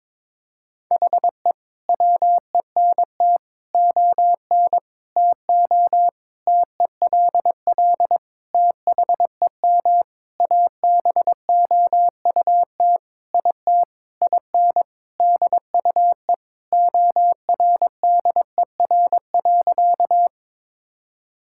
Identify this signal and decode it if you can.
Morse code, heard as a keyed tone, about 22 wpm, 700 Hz. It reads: HE WENT ON TO TELL THEM ABOUT IT IN DUE ORDER.
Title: Sentences from Top 1000 Words - Review Part 6 - 22wpm